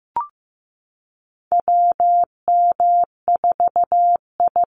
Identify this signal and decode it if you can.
Morse code, heard as a keyed tone, about 15 wpm, 700 Hz. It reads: WM4I
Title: Call Signs Repeated 3x - 15wpm